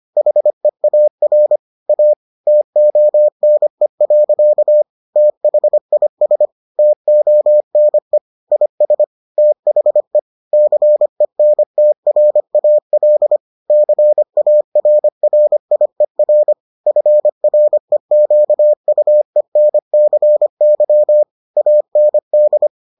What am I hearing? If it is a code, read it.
Morse code, heard as a keyed tone, about 25 wpm, 600 Hz. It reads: HEAR A TONE. THIS TONE IS THE CENTRAL CARRIER FREQUENCY AND